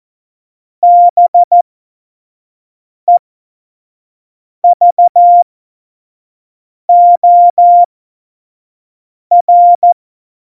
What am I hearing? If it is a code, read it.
Morse code, heard as a keyed tone, about 14 wpm, spaced as for 5 wpm, 700 Hz. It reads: BEVOR
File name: Morsekurs / Witze / Witz 23. Januar 2018, 25 F70